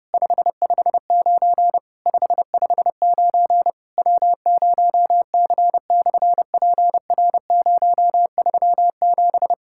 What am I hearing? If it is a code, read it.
Morse code, heard as a keyed tone, about 30 wpm, 700 Hz. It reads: 559 559 W0C/PR037
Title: SOTA QSO Practice 30wpm